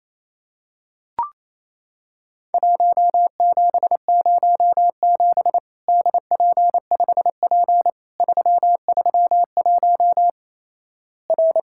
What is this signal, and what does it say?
Morse code, heard as a keyed tone, about 28 wpm, 700 Hz. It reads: 1707 DP5P 331 R
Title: Worked All Europe DX Contest - QTC Exchange - 28wpm